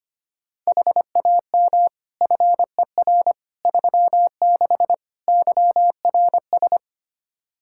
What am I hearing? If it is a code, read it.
Morse code, heard as a keyed tone, about 25 wpm, 700 Hz. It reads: HAM FER 36 YRS
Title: Practice QSO - Exchange 3 - More Info - 25wpm